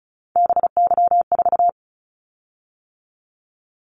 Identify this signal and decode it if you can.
Morse code, heard as a keyed tone, about 34 wpm, 700 Hz. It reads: BY4
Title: Callsign Prefixes for Top 100 DXCC - 35wpm